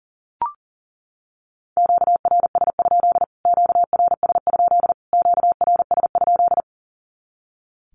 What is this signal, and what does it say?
Morse code, heard as a keyed tone, about 40 wpm, 700 Hz. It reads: QRS? QRS? QRS?
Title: Q-Codes Repeated 3x 40wpm